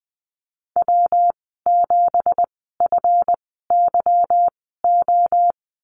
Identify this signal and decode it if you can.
Morse code, heard as a keyed tone, about 20 wpm, spaced as for 14 wpm, 700 Hz. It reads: W7FYO
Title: 193 Lesson 31g - Letter v - Call Signs 20wpm